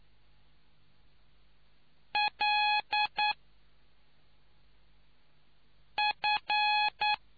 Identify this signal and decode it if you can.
Morse code, heard as a keyed tone, about 9 wpm, 840 Hz. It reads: L F